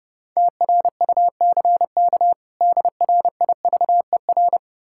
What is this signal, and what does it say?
Morse code, heard as a keyed tone, about 32 wpm, 700 Hz. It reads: TRUCK DRIVER
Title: K1USN SST QSO Practice 30wpm